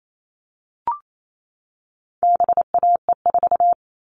Morse code, heard as a keyed tone, about 28 wpm, 700 Hz. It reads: BAE4
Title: All Cut Numbers - 4-Digit 28wpm